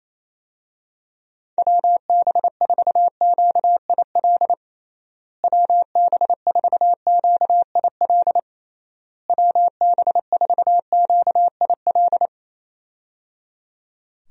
Morse code, heard as a keyed tone, about 28 wpm, 700 Hz. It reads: WB4QIL WB4QIL WB4QIL
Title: USA Call Signs - 6 Characters - Repeated 3x - 28wpm